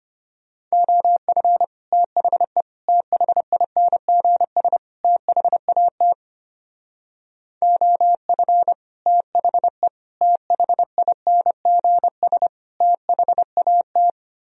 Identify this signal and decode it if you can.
Morse code, heard as a keyed tone, about 32 wpm, 700 Hz. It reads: OF THE THINGS THAT OF THE THINGS THAT